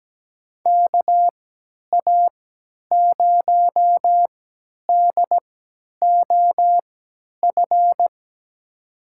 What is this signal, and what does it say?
Morse code, heard as a keyed tone, about 17 wpm, spaced as for 10 wpm, 700 Hz. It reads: KA0DOF
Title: USA Call Signs - 6 Characters - 3x Character Spacing - 17wpm